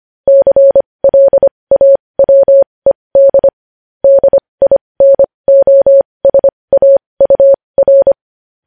Morse code, heard as a keyed tone, about 25 wpm, spaced as for 20 wpm, 550 Hz. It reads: CLAWED DINOSAUR
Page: news-headlines-in-morse-code-at-20-wpm